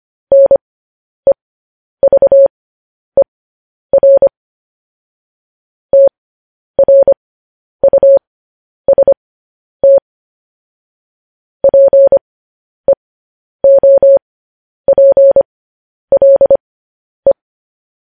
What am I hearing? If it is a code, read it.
Morse code, heard as a keyed tone, about 25 wpm, spaced as for 10 wpm, 550 Hz. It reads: NEVER TRUST PEOPLE
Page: news headlines in morse code at 10 wpm